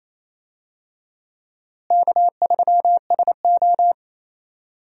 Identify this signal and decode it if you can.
Morse code, heard as a keyed tone, about 28 wpm, 700 Hz. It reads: K3SO
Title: USA Call Signs - 4 Characters - Repeated 3x - 28wpm